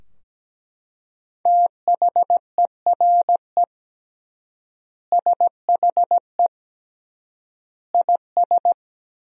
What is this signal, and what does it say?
Morse code, heard as a keyed tone, about 17 wpm, 700 Hz. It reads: THERE SHE IS
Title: 3x Word-Spacing - Sentences from Top 200 Words - 17wpm